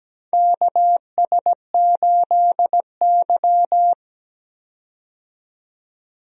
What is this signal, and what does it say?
Morse code, heard as a keyed tone, about 17 wpm, 700 Hz. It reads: KS8Y
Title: USA Call Signs - 4 Characters - 17wpm